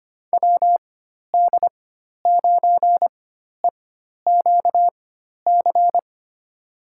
Morse code, heard as a keyed tone, about 25 wpm, spaced as for 12 wpm, 700 Hz. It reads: WD9EQC
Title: USA Call Signs - 6 Characters - 4x Character Spacing - 25wpm